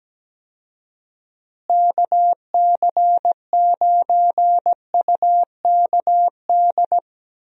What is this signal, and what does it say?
Morse code, heard as a keyed tone, about 17 wpm, 700 Hz. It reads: KC9UKD
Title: USA Call Signs - 6 Characters - Repeated 2x - 17wpm